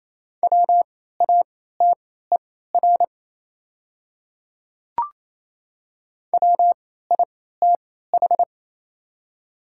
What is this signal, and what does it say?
Morse code, heard as a keyed tone, about 28 wpm, spaced as for 16 wpm, 700 Hz. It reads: WATER WITH
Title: 3x Character Spacing - Top 200 Words - 28wpm